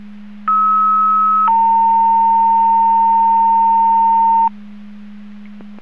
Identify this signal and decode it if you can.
Two-tone page, heard as a one-second tone, then a three-second tone, about 1290 then 910 Hz. It is heard from 0.5 to 4.5 s.
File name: Portage 154.385 Paging